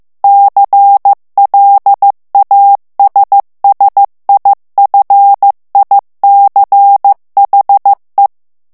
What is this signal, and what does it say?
Morse code, heard as a keyed tone, about 15 wpm, 800 Hz. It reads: CLASSIFICHE
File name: morse mp3 / ITALIEN texte clair 2 vitesse 14 mots minute